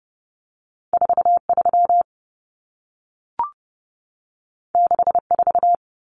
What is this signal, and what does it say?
Morse code, heard as a keyed tone, about 30 wpm, 700 Hz. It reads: 43 64
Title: Numbers - 2-Digit 30wpm